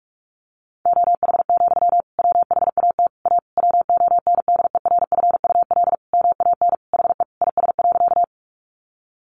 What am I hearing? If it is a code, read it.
Morse code, heard as a keyed tone, about 44 wpm, 700 Hz. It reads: OH, WHAT A WONDERFUL MAN HE IS.
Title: Sentences from Top 1100 Words 45wpm